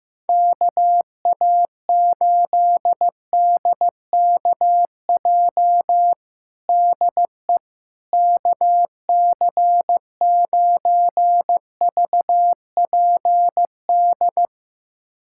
Morse code, heard as a keyed tone, about 15 wpm, 700 Hz. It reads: KA8DKJ DE KC9VPD
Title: USA Callsign DE Callsign - 6 Characters - 15wpm